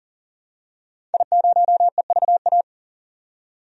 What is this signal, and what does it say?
Morse code, heard as a keyed tone, about 40 wpm, 700 Hz. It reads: I0EVA
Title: DXCC-100 Call Signs - 5 Characters - 40wpm